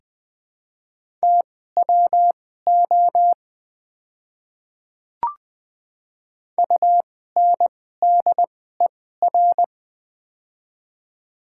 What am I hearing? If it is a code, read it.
Morse code, heard as a keyed tone, about 20 wpm, spaced as for 14 wpm, 700 Hz. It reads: TWO UNDER
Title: 2x Character Spacing - Top 300 Words - 20wpm